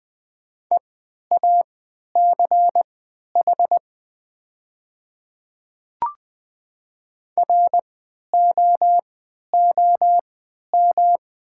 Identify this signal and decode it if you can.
Morse code, heard as a keyed tone, about 20 wpm, spaced as for 11 wpm, 700 Hz. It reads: EACH ROOM
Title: 3x Character Spacing - Top 300 Words - 20wpm